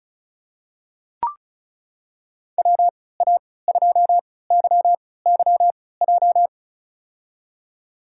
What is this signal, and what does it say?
Morse code, heard as a keyed tone, about 36 wpm, spaced as for 20 wpm, 690 Hz. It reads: WA2YYJ
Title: USA Call Signs - 3x Character Spacing - 35wpm